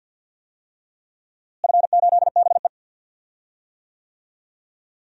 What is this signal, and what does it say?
Morse code, heard as a keyed tone, about 50 wpm, 700 Hz. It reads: F8BE
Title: DXCC-100 Call Signs - 4 Characters - Repeated 3x - 50wpm